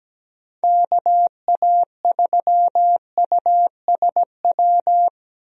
Morse code, heard as a keyed tone, about 17 wpm, 700 Hz. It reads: KA3USW
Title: USA Call Signs - 6 Characters - 17wpm